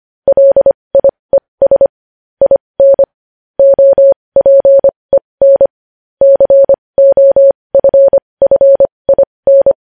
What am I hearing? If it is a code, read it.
Morse code, heard as a keyed tone, about 25 wpm, spaced as for 20 wpm, 550 Hz. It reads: LIES IN OPEN COFFIN
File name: podcast / News headlines-1x-25x20